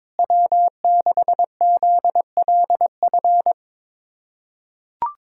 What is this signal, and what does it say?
Morse code, heard as a keyed tone, about 22 wpm, 700 Hz. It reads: W6ZLF
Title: USA Callsign DE Callsign - 5 Characters - 22wpm